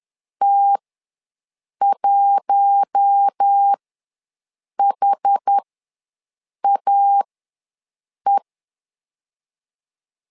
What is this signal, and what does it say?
Morse code, heard as a keyed tone, about 10 wpm, spaced as for 6 wpm, 800 Hz. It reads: T1HAE
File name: LEZIONI / 4 Q1 ws 10wpm 2538